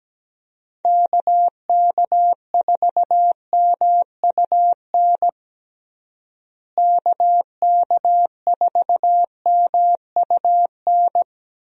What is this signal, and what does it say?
Morse code, heard as a keyed tone, about 17 wpm, 700 Hz. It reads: KK4MUN KK4MUN
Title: USA Call Signs - 6 Characters - Repeated 2x - 17wpm